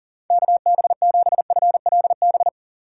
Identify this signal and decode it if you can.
Morse code, heard as a keyed tone, about 40 wpm, 700 Hz. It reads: KB7FLB
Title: USA Call Signs - 6 Characters - 3x Character Spacing - 40wpm